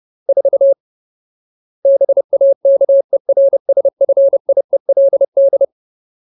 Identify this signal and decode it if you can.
Morse code, heard as a keyed tone, about 30 wpm, 550 Hz. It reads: 4 BAKERSFIELD
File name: Ultimate Word List / 30WPM Ultimate